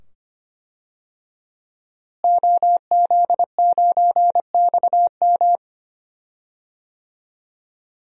Morse code, heard as a keyed tone, about 25 wpm, 700 Hz. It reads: OZ9XM
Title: DXCC-100 Call Signs - 5 Characters - Repeated 2x - 25wpm